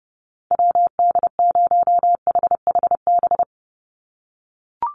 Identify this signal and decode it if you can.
Morse code, heard as a keyed tone, about 30 wpm, 700 Hz. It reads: WD0HHB
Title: USA Call Signs 30wpm